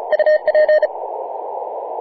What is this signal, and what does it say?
Morse code, heard as a keyed tone, about 34 wpm, 610 Hz, with radio noise. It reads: UP
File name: posts / feb-28-0416z-3g0zc-21-020mhz-cw